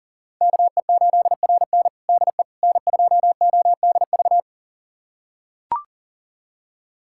Morse code, heard as a keyed tone, about 37 wpm, 700 Hz. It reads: KE8RN DE N2ODV
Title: USA Callsign DE Callsign - 5 Characters - 40wpm